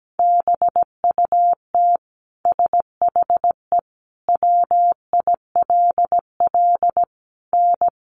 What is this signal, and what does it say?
Morse code, heard as a keyed tone, about 17 wpm, 700 Hz. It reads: BUT SHE WILL N
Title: Sentences from Top 1200 Words 17wpm